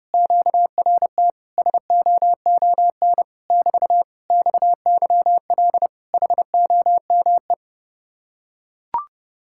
Morse code, heard as a keyed tone, about 30 wpm, 700 Hz. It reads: QRT SOON <BT> XYL HOME